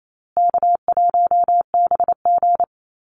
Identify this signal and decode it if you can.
Morse code, heard as a keyed tone, about 28 wpm, 700 Hz. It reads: K1BG
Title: Easy Call Signs Repeated 3x - 28wpm